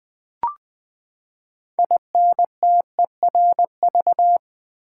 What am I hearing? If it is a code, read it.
Morse code, heard as a keyed tone, about 20 wpm, 700 Hz. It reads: INTERV
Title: Build A Word - 6 Letters At A Time - 12-Letter Words - 20wpm